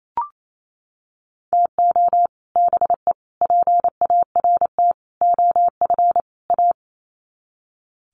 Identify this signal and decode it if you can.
Morse code, heard as a keyed tone, about 28 wpm, 700 Hz. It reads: TO BE PART OF A
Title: Sets of 5 Words Encore2 28wpm